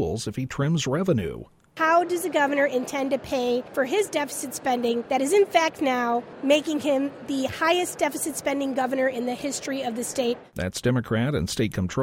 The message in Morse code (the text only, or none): none